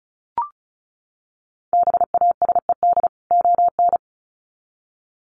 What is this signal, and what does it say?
Morse code, heard as a keyed tone, about 34 wpm, 700 Hz. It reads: BASED ON
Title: Sets of 2 Words 35wpm